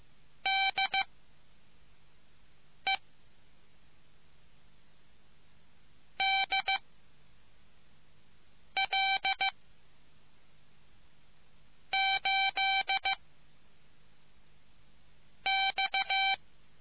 Morse code, heard as a keyed tone, about 15 wpm, 780 Hz. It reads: D E D L 8 X